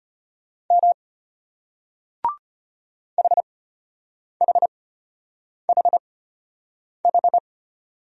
Morse code, heard as a keyed tone, about 34 wpm, spaced as for 8 wpm, 700 Hz. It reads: M HHHH